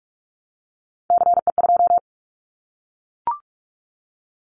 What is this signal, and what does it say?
Morse code, heard as a keyed tone, about 44 wpm, 700 Hz. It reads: CE2
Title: Callsign Prefixes for Top 100 DXCC - 45wpm